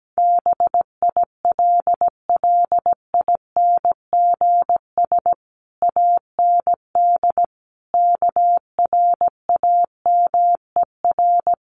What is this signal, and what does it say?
Morse code, heard as a keyed tone, about 17 wpm, 700 Hz. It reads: BILLINGS AND KRAMER
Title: A Sound Of Thunder 17wpm